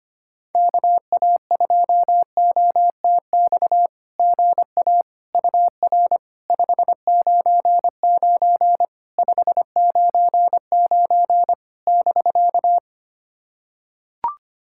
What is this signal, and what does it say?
Morse code, heard as a keyed tone, about 25 wpm, 700 Hz. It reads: KA2OTX GA UR 599 599 <BK>
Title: SOTA QSO Practice 25wpm